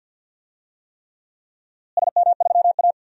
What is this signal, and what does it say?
Morse code, heard as a keyed tone, about 53 wpm, 700 Hz. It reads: SM3A
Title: DXCC-100 Call Signs - 4 Characters - Repeated 3x - 50wpm